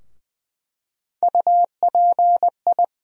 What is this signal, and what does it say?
Morse code, heard as a keyed tone, about 20 wpm, 700 Hz. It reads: UPI
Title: USA Call Sign Postfixes - 3 Letters 20wpm